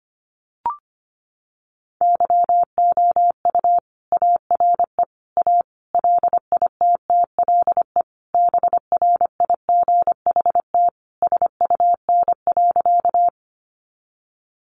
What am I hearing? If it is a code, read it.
Morse code, heard as a keyed tone, about 25 wpm, 700 Hz. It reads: YOU ARE A LITTLE BRIGHT SUN.
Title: Sentences from Top 1000 Words 25wpm